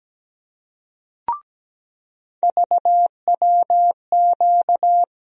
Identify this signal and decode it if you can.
Morse code, heard as a keyed tone, about 17 wpm, 700 Hz. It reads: VWQ
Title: USA Call Sign Postfixes - 3 Letters 17wpm